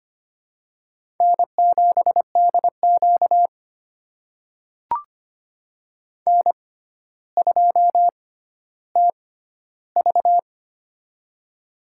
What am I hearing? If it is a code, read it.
Morse code, heard as a keyed tone, about 25 wpm, spaced as for 13 wpm, 700 Hz. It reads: N7DQ N2TV
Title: USA Call Signs - 4 Characters - 6x Character Spacing - 25wpm